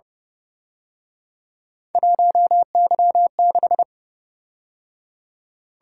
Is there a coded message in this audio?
Yes: Morse code from 1.9 to 3.8 s, heard as a keyed tone, about 30 wpm, 700 Hz. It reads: 1Y6